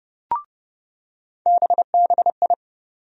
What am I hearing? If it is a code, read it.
Morse code, heard as a keyed tone, about 30 wpm, 700 Hz. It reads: BBI
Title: USA Call Sign Postfixes - 3 Letters 30wpm